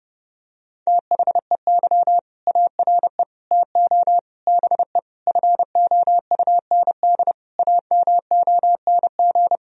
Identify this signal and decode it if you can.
Morse code, heard as a keyed tone, about 30 wpm, 700 Hz. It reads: THEY ARE TO BE FOUND AMONG
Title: Sentences from Top 1000 Words - Review Part 3 - 30wpm